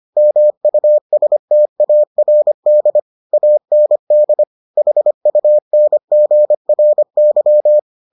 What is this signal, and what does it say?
Morse code, heard as a keyed tone, about 25 wpm, 600 Hz. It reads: MUSTARD AND HUNGRY